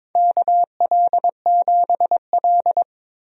Morse code, heard as a keyed tone, about 22 wpm, 700 Hz. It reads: KL7L
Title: DXCC-100 Call Signs - 4 Characters - 22wpm